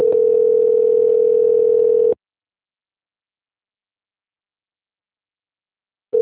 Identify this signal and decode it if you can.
Ringback tone (North America), 440 + 480 Hz.
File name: rg-1600-unknown- - - .14655